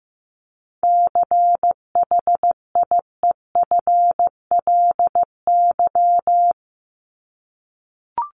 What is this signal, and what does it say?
Morse code, heard as a keyed tone, about 15 wpm, 700 Hz. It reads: CHIEFLY
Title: LY Suffix - Top 100 Words with Base Words 15wpm